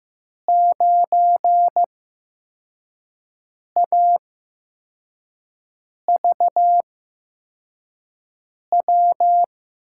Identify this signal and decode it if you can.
Morse code, heard as a keyed tone, about 15 wpm, spaced as for 4 wpm, 700 Hz. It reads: 9AVW
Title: USA Call Signs - 5 Characters - 8x Character Spacing - 15wpm